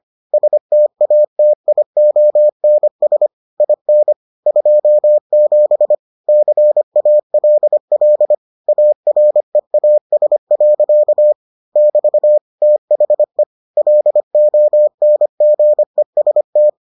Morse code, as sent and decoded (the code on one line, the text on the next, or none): ... - .- - .. --- -. ... / .. -. / ..--- --... / -.-. .- .-.. .-.. / .- .-. .- ... .-.-.- / -...- / - .... / .-.. --- -. --. ... -
STATIONS IN 27 CALL AREAS. <BT> THE LONGEST